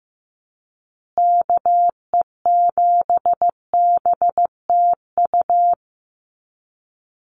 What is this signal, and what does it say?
Morse code, heard as a keyed tone, about 15 wpm, 700 Hz. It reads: KE7BTU
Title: USA Call Signs Repeated 2x 15wpm